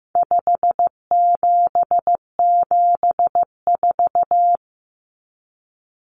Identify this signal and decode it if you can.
Morse code, heard as a keyed tone, about 15 wpm, 700 Hz. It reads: 5774
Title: Common Cut Numbers - 4-Digit 15wpm